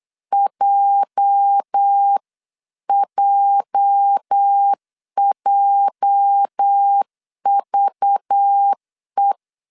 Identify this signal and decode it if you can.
Morse code, heard as a keyed tone, about 8 wpm, 800 Hz. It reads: JJJVE
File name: LEZIONI / 8 YJ45 / 8 YJ 8wpm 3313